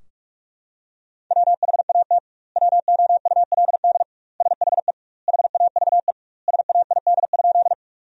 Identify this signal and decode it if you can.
Morse code, heard as a keyed tone, about 41 wpm, 700 Hz. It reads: WHAT WOULD SHE HAVE SAID?
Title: 2x Word-Spacing - Sentences from Top 100 Words - 45wpm